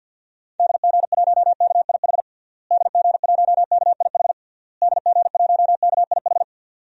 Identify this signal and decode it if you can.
Morse code, heard as a keyed tone, about 49 wpm, 700 Hz. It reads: DG1KIH DG1KIH DG1KIH
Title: DXCC-100 Call Signs - 6 Characters - Repeated 3x - 50wpm